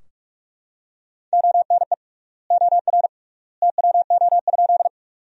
Morse code, heard as a keyed tone, about 46 wpm, 690 Hz. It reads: ONE OR TWO?
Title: 3x Word-Spacing - Sentences from Top 100 Words - 45wpm